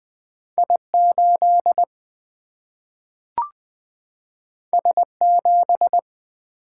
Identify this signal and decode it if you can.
Morse code, heard as a keyed tone, about 20 wpm, 700 Hz. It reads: I8 S7
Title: Single-Letter Single-Number 20wpm